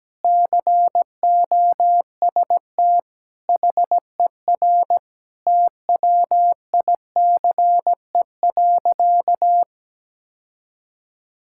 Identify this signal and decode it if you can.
Morse code, heard as a keyed tone, about 17 wpm, 700 Hz. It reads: COST HER TWICE.
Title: Sentences from Top 1600 Words 17wpm